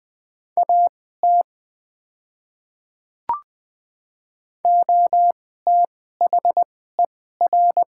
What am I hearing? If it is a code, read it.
Morse code, heard as a keyed tone, about 20 wpm, spaced as for 14 wpm, 700 Hz. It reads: AT OTHER
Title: 2x Character Spacing - Top 100 Words - 20wpm